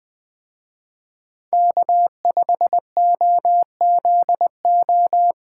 Morse code, heard as a keyed tone, about 20 wpm, 700 Hz. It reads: K5OZO
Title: Speed Racing - USA 1x3 Call Signs 17wpm